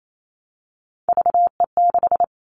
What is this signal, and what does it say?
Morse code, heard as a keyed tone, about 28 wpm, 700 Hz. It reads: VE6
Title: All Cut Numbers - 3-Digit 28wpm